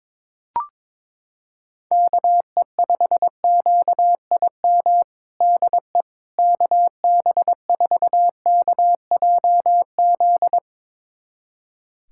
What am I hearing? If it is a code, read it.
Morse code, heard as a keyed tone, about 22 wpm, 700 Hz. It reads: KE5QIM DE KB4KJZ
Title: USA Callsign DE Callsign - 6 Characters - 22wpm